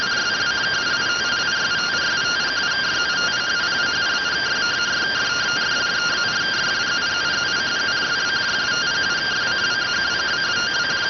Two-tone teleprinter signal, about 170 Hz shift.